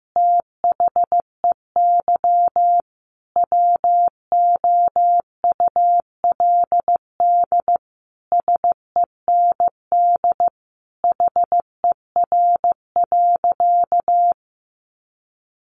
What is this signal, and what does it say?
Morse code, heard as a keyed tone, about 15 wpm, 700 Hz. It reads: THEY WOULD SEND HER.
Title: Sentences from Top 1500 Words 15wpm